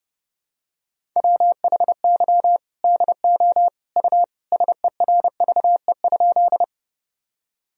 Morse code, heard as a keyed tone, about 32 wpm, 700 Hz. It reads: WHY DO U SERVE?
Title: Sentences from Top 400 Words and Abbreviations 30wpm